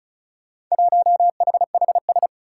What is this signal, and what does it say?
Morse code, heard as a keyed tone, about 36 wpm, 700 Hz. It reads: 1HHS
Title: Four Challenging Letter-Number Combinations 35wpm